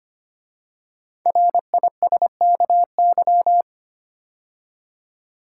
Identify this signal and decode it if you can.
Morse code, heard as a keyed tone, about 25 wpm, 700 Hz. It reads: RISKY
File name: 5 Letter Words - First 3 Letters Given - 25wpm